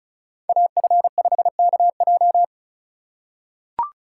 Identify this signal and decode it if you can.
Morse code, heard as a keyed tone, about 36 wpm, 700 Hz. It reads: AF5KJ
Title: USA Call Signs - 3x Character Spacing - 35wpm